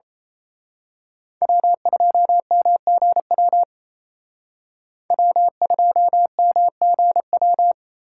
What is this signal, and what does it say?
Morse code, heard as a keyed tone, about 30 wpm, 700 Hz. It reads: W2MGW W2MGW
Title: Speed Racing - USA 1x3 Call Signs 28wpm